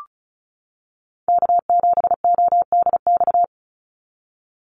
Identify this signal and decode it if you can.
Morse code, heard as a keyed tone, about 34 wpm, 700 Hz. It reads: K7ODX
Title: NA SOTA Activator Callsigns 35wpm